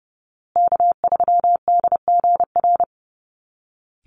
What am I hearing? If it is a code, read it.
Morse code, heard as a keyed tone, about 30 wpm, 700 Hz. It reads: K3DGR
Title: Call Signs Repeated 2x - 30wpm